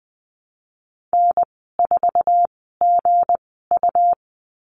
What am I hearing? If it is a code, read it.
Morse code, heard as a keyed tone, about 20 wpm, spaced as for 15 wpm, 700 Hz. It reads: N4GU
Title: 213 Lesson 34g - Number 8 - Call Signs 20wpm